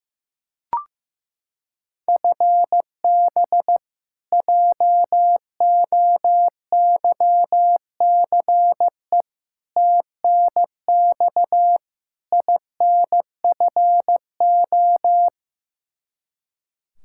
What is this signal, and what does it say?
Morse code, heard as a keyed tone, about 16 wpm, 700 Hz. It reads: FB JOYCE TNX INFO
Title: K1USN SST QSO Practice 15wpm